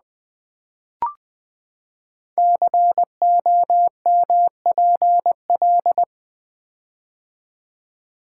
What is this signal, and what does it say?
Morse code, heard as a keyed tone, about 20 wpm, 700 Hz. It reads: COMPL
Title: Build A Word - 5 Letters At A Time - 10-Letter Words - 20wpm